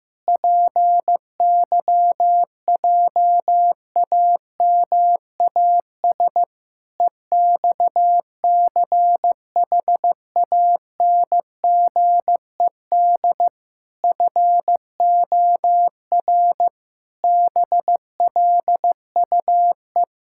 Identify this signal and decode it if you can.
Morse code, heard as a keyed tone, about 15 wpm, 700 Hz. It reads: PYJAMAS EXCHANGED FOR BLUE
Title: Pangram Challenge v2 - 15wpm